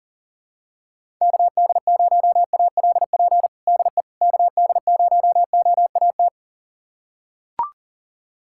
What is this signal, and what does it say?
Morse code, heard as a keyed tone, about 40 wpm, 700 Hz. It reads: KD0ALP DE KD0OAT
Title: USA Callsign DE Callsign 40wpm